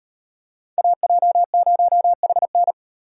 Morse code, heard as a keyed tone, about 30 wpm, 700 Hz. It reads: AJ0HN
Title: Speed Racing - USA 2x2 Call Signs 25wpm